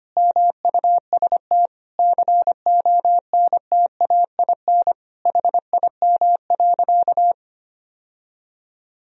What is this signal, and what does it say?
Morse code, heard as a keyed tone, about 25 wpm, 700 Hz. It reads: MUST CONTAIN HIM.
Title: Sentences from Top 1000 Words - Review Part 2 - 25wpm